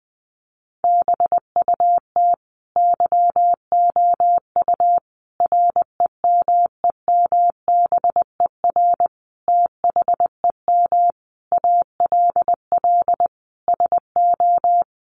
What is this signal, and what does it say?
Morse code, heard as a keyed tone, about 20 wpm, 700 Hz. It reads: BUT YOU REMEMBER THEM ALL SO
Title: Sentences from Top 1300 Words 20wpm